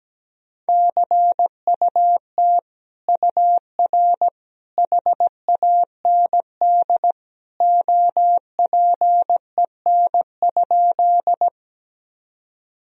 Morse code, heard as a keyed tone, about 17 wpm, 700 Hz. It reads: CUT UR HAND OPEN?
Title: Sentences from Top 300 Words and Abbreviations 17wpm